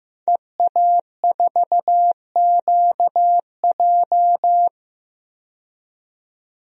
Morse code, heard as a keyed tone, about 15 wpm, 700 Hz. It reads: EA4QJ
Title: DXCC-100 Call Signs - 5 Characters - 15wpm